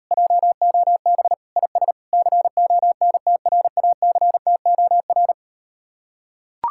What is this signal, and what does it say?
Morse code, heard as a keyed tone, about 38 wpm, 700 Hz. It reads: JOB IS CONTRACTOR